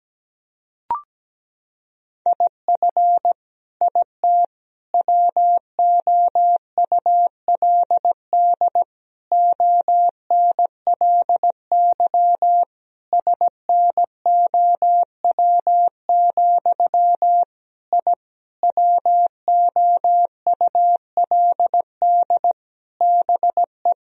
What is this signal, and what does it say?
Morse code, heard as a keyed tone, about 17 wpm, 700 Hz. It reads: IF IT WOULD ONLY SNOW, I WOULD BE
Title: Sentences from Top 1000 Words - Review Part 3 - 17wpm